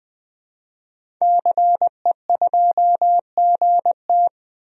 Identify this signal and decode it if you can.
Morse code, heard as a keyed tone, about 21 wpm, 700 Hz. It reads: CE2GT
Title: DXCC-100 Call Signs - 5 Characters - 20wpm